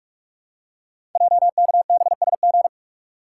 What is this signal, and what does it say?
Morse code, heard as a keyed tone, about 44 wpm, 700 Hz. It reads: JKBSG
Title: Code Groups - 5 Letters - 45wpm